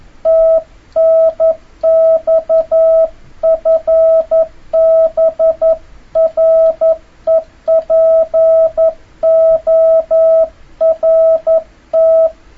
Morse code, heard as a keyed tone, about 11 wpm, 650 Hz. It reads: TNX FB REPORT
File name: jh8cbh / 2026cw